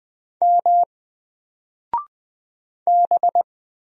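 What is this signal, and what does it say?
Morse code, heard as a keyed tone, about 20 wpm, 700 Hz. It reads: M B
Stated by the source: Single letter and number Morse drill at 20 wpm, from N4LCD.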